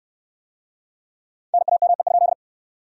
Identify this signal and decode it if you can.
Morse code, heard as a keyed tone, about 61 wpm, 700 Hz. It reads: NAME?